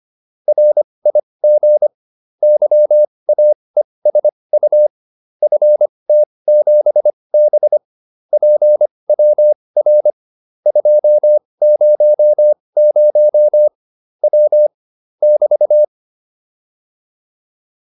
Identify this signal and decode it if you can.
Morse code, heard as a keyed tone, about 25 wpm, spaced as for 20 wpm, 600 Hz. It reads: RIG YAESU FT7B PWR 200 W <BT>